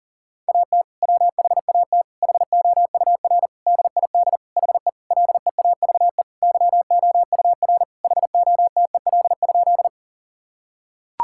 Morse code, heard as a keyed tone, about 40 wpm, 700 Hz. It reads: AT WHAT HOUR DID HE LEAVE YOUR HOTEL?
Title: Sentences from Top 1600 Words 40wpm